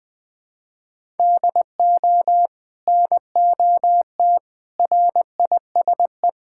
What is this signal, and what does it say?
Morse code, heard as a keyed tone, about 21 wpm, 700 Hz. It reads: DO NOT RISE
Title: Sentences from Top 1000 Words - Review Part 6 - 20wpm